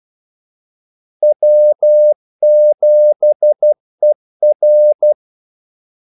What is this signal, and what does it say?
Morse code, heard as a keyed tone, about 12 wpm, 600 Hz. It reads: W7ER
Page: Kaldesignaler 12 wpm